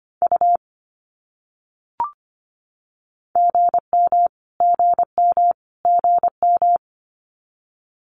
Morse code, heard as a keyed tone, about 25 wpm, 700 Hz. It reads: U GM GM GM
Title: Abbreviations Repeated 3x - 25wpm